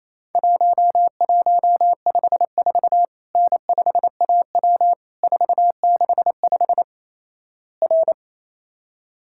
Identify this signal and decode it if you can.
Morse code, heard as a keyed tone, about 28 wpm, 700 Hz. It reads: 1154 N5AW 465 R
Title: Worked All Europe DX Contest - QTC Exchange - 28wpm